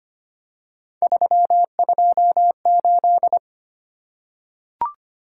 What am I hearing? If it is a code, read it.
Morse code, heard as a keyed tone, about 25 wpm, 700 Hz. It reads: 328
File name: Three Letter-Number Combinations - 25wpm